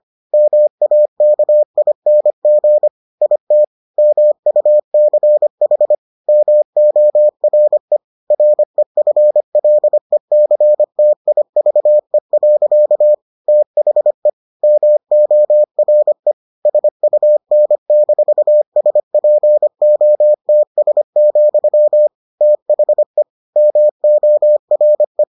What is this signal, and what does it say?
Morse code, heard as a keyed tone, about 25 wpm, 600 Hz. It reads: MAKING IT MUCH MORE REFLECTIVE. THE MORE SUN-SPOTS, THE MORE